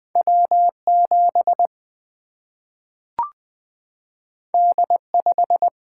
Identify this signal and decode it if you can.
Morse code, heard as a keyed tone, about 20 wpm, 700 Hz. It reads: W 7 D 5
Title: Single-Letter Single-Number 20wpm